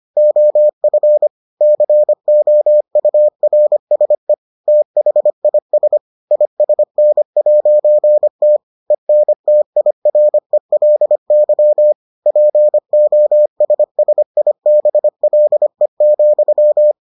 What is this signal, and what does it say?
Morse code, heard as a keyed tone, about 25 wpm, 600 Hz. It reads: OF COURSE THIS ISN'T ENTIRELY POSSIBLE,